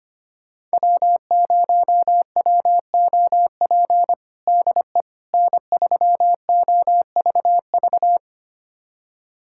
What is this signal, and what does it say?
Morse code, heard as a keyed tone, about 25 wpm, 700 Hz. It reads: W0WOP DE N3OVV
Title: USA Callsign DE Callsign - 5 Characters - 25wpm